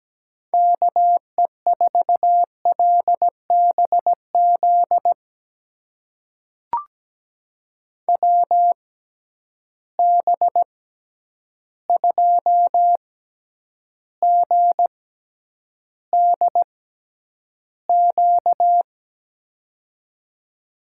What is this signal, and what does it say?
Morse code, heard as a keyed tone, about 17 wpm, spaced as for 9 wpm, 700 Hz. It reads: KE4LBZ WB2GDQ
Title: USA Call Signs - 6 Characters - 6x Character Spacing - 17wpm